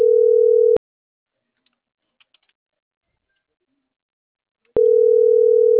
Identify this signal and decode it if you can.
Ringback tone (North America), 440 + 480 Hz.